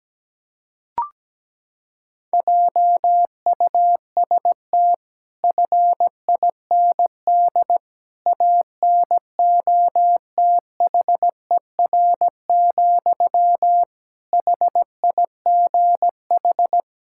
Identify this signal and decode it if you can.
Morse code, heard as a keyed tone, about 17 wpm, 700 Hz. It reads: JUST FIND ANOTHER, HIGH
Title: Sentences from Top 1600 Words 17wpm